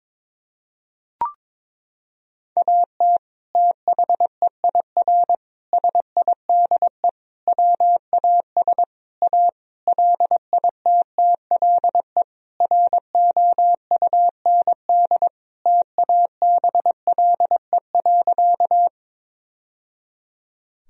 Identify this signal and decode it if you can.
Morse code, heard as a keyed tone, about 22 wpm, 700 Hz. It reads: AT THEIR SIDE WAS A LITTLE ROUND TABLE.
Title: Sentences from Top 1000 Words - Review Part 2 - 22wpm